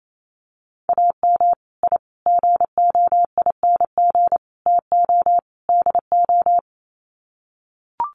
Morse code, heard as a keyed tone, about 28 wpm, 700 Hz. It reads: AM I GOING TO DO
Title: Sets of 5 Words Encore 28wpm